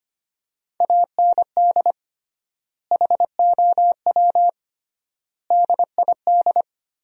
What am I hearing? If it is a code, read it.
Morse code, heard as a keyed tone, about 25 wpm, 700 Hz. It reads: AND HOW DID
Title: 3x Word-Spacing - Sentences from Top 200 Words - 25wpm